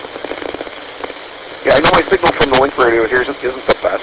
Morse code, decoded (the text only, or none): none